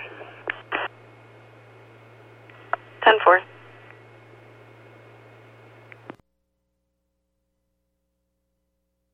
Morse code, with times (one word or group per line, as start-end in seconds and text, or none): none